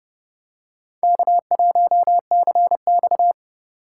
Morse code, heard as a keyed tone, about 32 wpm, 700 Hz. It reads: K1CX
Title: K1USN SST QSO Practice - 30wpm